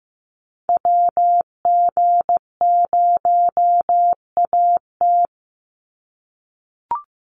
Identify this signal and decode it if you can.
Morse code, heard as a keyed tone, about 15 wpm, 700 Hz. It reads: WG0AT
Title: NA SOTA Chaser Callsigns 15wpm